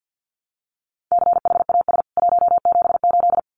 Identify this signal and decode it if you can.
Morse code, heard as a keyed tone, about 50 wpm, 700 Hz. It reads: CHAS 178
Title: CWT Exchange 50wpm